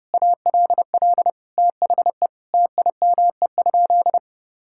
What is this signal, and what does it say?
Morse code, heard as a keyed tone, about 30 wpm, 700 Hz. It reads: ALL THE TIME?
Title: Sentences from Top 1000 Words - Review Part 5 - 30wpm